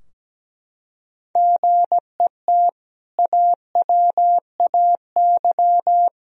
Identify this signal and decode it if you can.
Morse code, heard as a keyed tone, about 17 wpm, 700 Hz. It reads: GET AWAY
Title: Sentences from Top 1000 Words - Review Part 2 - 17wpm